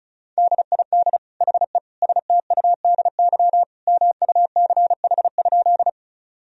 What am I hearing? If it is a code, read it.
Morse code, heard as a keyed tone, about 34 wpm, 700 Hz. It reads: DID HE STUDY MUCH?
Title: Sentences from Top 1000 Words - Review Part 1 - 35wpm